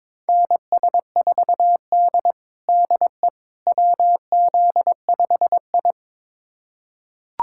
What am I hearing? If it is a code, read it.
Morse code, heard as a keyed tone, about 22 wpm, 700 Hz. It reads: NS4D DE WZ5I
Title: USA Callsign DE Callsign - 4 Characters - 22wpm